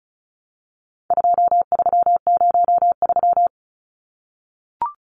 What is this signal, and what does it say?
Morse code, heard as a keyed tone, about 34 wpm, 700 Hz. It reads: 2303
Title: Numbers - 4-Digit 35wpm